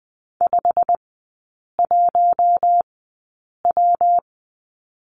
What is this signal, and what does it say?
Morse code, heard as a keyed tone, about 20 wpm, spaced as for 8 wpm, 700 Hz. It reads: 51W
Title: 178 Lesson 29e - Slash - Three Characters 20wpm